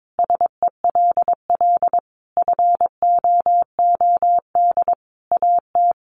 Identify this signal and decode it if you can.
Morse code, heard as a keyed tone, about 22 wpm, 700 Hz. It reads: SELL FOOD AT
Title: Sentences from Top 1100 Words 22wpm